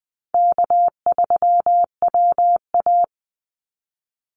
Morse code, heard as a keyed tone, about 20 wpm, 700 Hz. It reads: K3WA
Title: Call Signs Repeated 3x - 20wpm